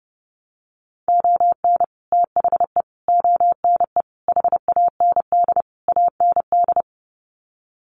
Morse code, heard as a keyed tone, about 30 wpm, 700 Hz. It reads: ON THE ONE HAND AND
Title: Sets of 5 Words Encore2 30wpm